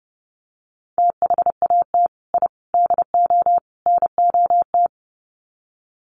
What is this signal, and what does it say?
Morse code, heard as a keyed tone, about 28 wpm, 700 Hz. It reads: THAT I DO NOT
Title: Sets of 4 Words 30wpm